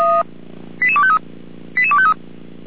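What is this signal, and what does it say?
Five-tone selective call (ZVEI), heard as quick runs of five tones, tones of about 70 ms. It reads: 88253 88253
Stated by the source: ZVEI five-tone selective call received on SDR.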